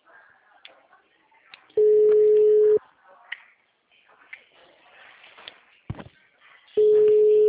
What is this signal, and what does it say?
Ringback tone (Europe, 425 Hz).
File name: out- -2019